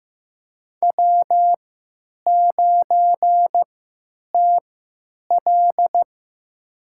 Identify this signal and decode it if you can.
Morse code, heard as a keyed tone, about 16 wpm, spaced as for 9 wpm, 700 Hz. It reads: W9TL
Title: USA Call Signs - 3x Character Spacing - 15wpm